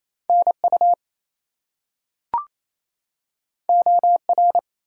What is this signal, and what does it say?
Morse code, heard as a keyed tone, about 28 wpm, 700 Hz. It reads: NU OR
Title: US States and Canadian Provinces 28wpm